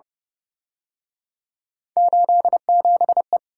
Morse code, heard as a keyed tone, about 29 wpm, 700 Hz. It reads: 87E